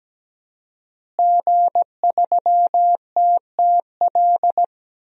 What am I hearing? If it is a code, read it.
Morse code, heard as a keyed tone, about 17 wpm, 700 Hz. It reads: G3TTL
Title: DXCC-100 Call Signs - 5 Characters - Repeated 3x - 17wpm